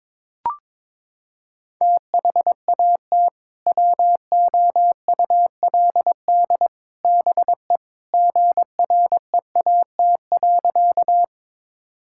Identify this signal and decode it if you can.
Morse code, heard as a keyed tone, about 22 wpm, 700 Hz. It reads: THAT WOULD BE GREAT.
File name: practice / 22wpm / Sentences with 2-Word Phrases - 22wpm